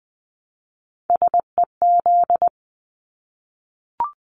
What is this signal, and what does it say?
Morse code, heard as a keyed tone, about 20 wpm, 700 Hz. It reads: SEZ
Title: Abbreviations Repeated 3x - 20wpm